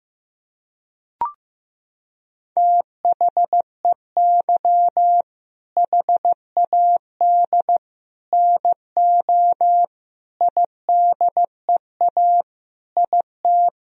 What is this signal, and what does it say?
Morse code, heard as a keyed tone, about 15 wpm, 700 Hz. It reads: THEY HAD NO IDEA IT
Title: Sentences from Top 1000 Words - Review Part 1 - 15wpm